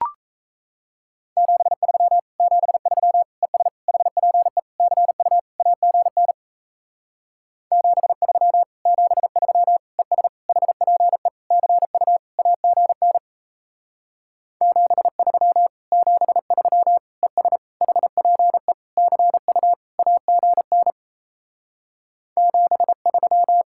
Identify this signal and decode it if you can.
Morse code, heard as a keyed tone, about 38 wpm, 700 Hz. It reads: 73 73 ES HPE CU AGN 73 73 ES HPE CU AGN 73 73 ES HPE CU AGN 73